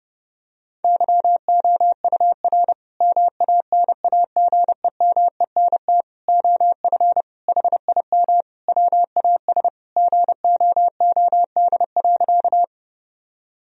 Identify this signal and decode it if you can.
Morse code, heard as a keyed tone, about 32 wpm, 700 Hz. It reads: YOUR MANAGEMENT OF HIM WAS GOOD.
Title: Sentences from Top 1000 Words - Review Part 4 - 30wpm